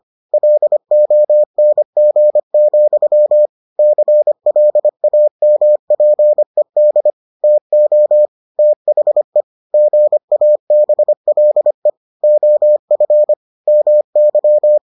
Morse code, heard as a keyed tone, about 25 wpm, 600 Hz. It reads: LONG, CLAMPED TO THE GABLE OF MY